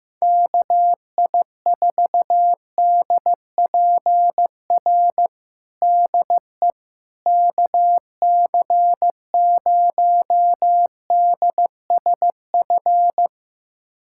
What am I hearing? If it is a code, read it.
Morse code, heard as a keyed tone, about 15 wpm, 700 Hz. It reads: KI4DPR DE KC0DSF
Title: USA Callsign DE Callsign 15wpm